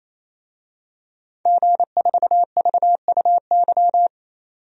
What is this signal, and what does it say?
Morse code, heard as a keyed tone, about 28 wpm, 700 Hz. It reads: G4VUY
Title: DXCC-100 Call Signs - 5 Characters - 28wpm